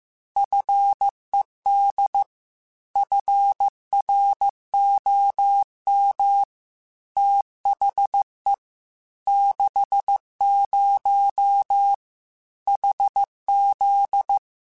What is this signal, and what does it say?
Morse code, heard as a keyed tone, about 15 wpm, 800 Hz. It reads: FED FROM THE 60 HZ
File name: Technical Subjects / Technical Subjects-mp3s / Sw Pwr Sup @13